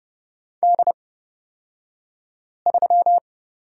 Morse code, heard as a keyed tone, about 31 wpm, spaced as for 5 wpm, 700 Hz. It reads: D3
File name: Lesson 36d - BT - Two Characters 30wpm